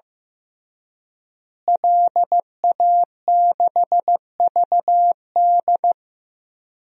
Morse code, heard as a keyed tone, about 15 wpm, 700 Hz. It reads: LA6VD
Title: DXCC-100 Call Signs - 5 Characters - Repeated 3x - 15wpm